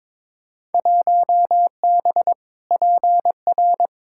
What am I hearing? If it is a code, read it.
Morse code, heard as a keyed tone, about 22 wpm, 700 Hz. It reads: 1B PR
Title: Field Day Exchanges Encore - 22wpm